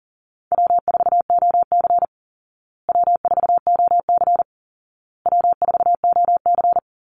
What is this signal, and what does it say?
Morse code, heard as a keyed tone, about 40 wpm, 700 Hz. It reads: W4OC W4OC W4OC
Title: Easy Call Signs Repeated 3x - 40wpm